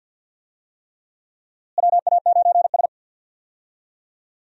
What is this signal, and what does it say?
Morse code, heard as a keyed tone, about 51 wpm, 700 Hz. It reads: WA9S
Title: USA Call Signs - 4 Characters - Repeated 3x - 50wpm